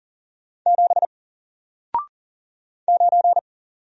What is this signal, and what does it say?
Morse code, heard as a keyed tone, about 40 wpm, 700 Hz. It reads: 7 9
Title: Single Letter-Number Alt 40wpm